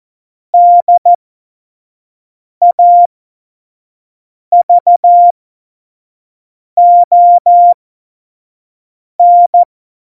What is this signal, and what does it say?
Morse code, heard as a keyed tone, about 14 wpm, spaced as for 5 wpm, 700 Hz. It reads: DAVON
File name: Morsekurs / Witze / Witz 19 Mai 2020, 25 F70